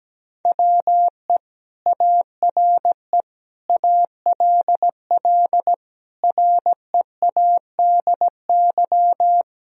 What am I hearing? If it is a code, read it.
Morse code, heard as a keyed tone, about 17 wpm, 700 Hz. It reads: WE ARE ALL READY
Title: Sentences from Top 1000 Words - Review Part 4 - 17wpm